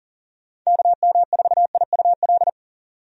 Morse code, heard as a keyed tone, about 40 wpm, 700 Hz. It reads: KM4IUL
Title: USA Call Signs - 6 Characters - 40wpm